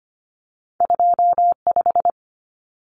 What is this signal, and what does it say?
Morse code, heard as a keyed tone, about 25 wpm, 700 Hz. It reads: 25
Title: Common Cut Numbers - 2-Digit 25wpm